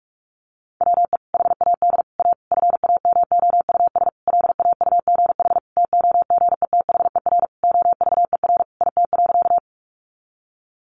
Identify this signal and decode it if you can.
Morse code, heard as a keyed tone, about 44 wpm, 700 Hz. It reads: WE HAD A FAMOUS LAUGH TOGETHER OVER IT.
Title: Sentences from Top 1500 Words 45wpm